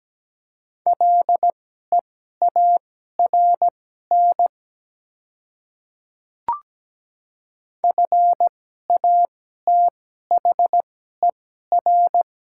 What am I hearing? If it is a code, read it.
Morse code, heard as a keyed tone, about 17 wpm, spaced as for 12 wpm, 700 Hz. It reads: LEARN FATHER
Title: 2x Character Spacing - Top 300 Words - 17wpm